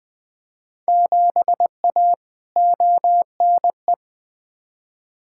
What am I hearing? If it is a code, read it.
Morse code, heard as a keyed tone, about 20 wpm, 700 Hz. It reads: 7A ONE
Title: Field Day Exchanges Encore - 20wpm